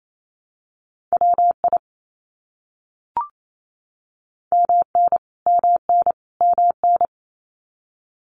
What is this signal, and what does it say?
Morse code, heard as a keyed tone, about 28 wpm, 700 Hz. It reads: WI MN MN MN
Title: US States Repeated 3x 28wpm